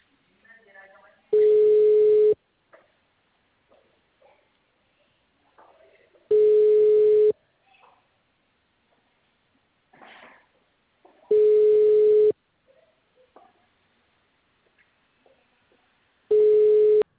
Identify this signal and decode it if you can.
Ringback tone (Europe, 425 Hz), with speech.
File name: out- -2009- - - .22704